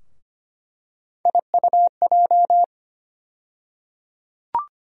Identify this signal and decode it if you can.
Morse code, heard as a keyed tone, about 25 wpm, 700 Hz. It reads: IUJ
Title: USA Call Sign Postfixes - 3 Letters 25wpm